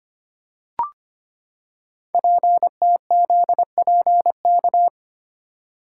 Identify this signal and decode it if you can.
Morse code, heard as a keyed tone, about 25 wpm, 700 Hz. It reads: PTZPK
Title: Code Groups - 5 Letters - 25wpm